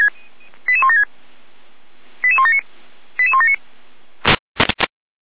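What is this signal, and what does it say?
Five-tone selective call (ZVEI), heard as quick runs of five tones, tones of about 70 ms. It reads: 88176 88179 88179
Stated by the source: ZVEI five-tone selective call received by SDR.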